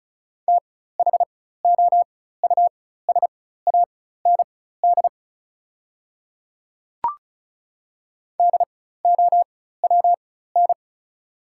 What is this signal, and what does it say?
Morse code, heard as a keyed tone, about 34 wpm, spaced as for 16 wpm, 690 Hz. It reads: THOUSAND DOWN
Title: 4x Character Spacing - Top 500 Words - 35wpm